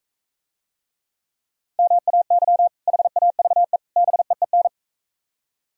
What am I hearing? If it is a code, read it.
Morse code, heard as a keyed tone, about 43 wpm, 700 Hz. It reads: MAY HAVE BEEN